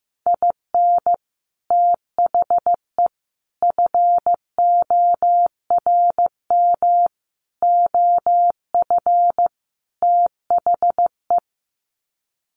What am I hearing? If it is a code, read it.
Morse code, heard as a keyed tone, about 15 wpm, 700 Hz. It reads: IN THE FORM OF THE
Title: Sets of 5 Words Encore2 15wpm